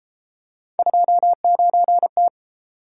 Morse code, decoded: 29T

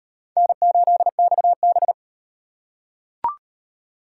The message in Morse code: N8XB

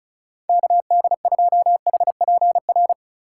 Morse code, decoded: KD2HPR

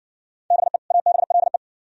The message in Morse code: BE ABLE

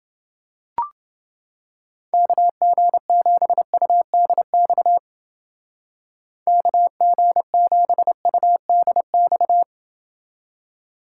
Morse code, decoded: KG7UDX KG7UDX

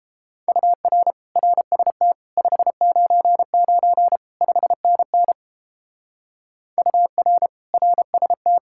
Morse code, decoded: UR RST 599 5NN UR RST

